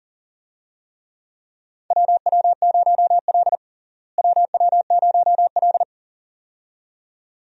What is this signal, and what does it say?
Morse code, heard as a keyed tone, about 40 wpm, 700 Hz. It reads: WW0L WW0L
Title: USA Call Signs - 4 Characters - Repeated 2x - 40wpm